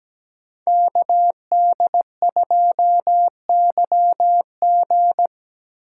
Morse code, heard as a keyed tone, about 20 wpm, 700 Hz. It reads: KD2YG
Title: Speed Racing - USA 2x2 Call Signs 17wpm